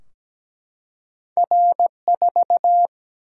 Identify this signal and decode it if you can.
Morse code, heard as a keyed tone, about 17 wpm, 700 Hz. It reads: R4
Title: Single-Letter Single-Number 17wpm